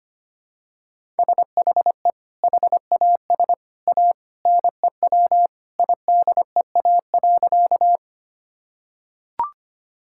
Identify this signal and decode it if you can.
Morse code, heard as a keyed tone, about 25 wpm, 700 Hz. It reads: SHE HAS A NEW IDEA.
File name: practice / 25wpm / Sentences with 2-Word Phrases - 25wpm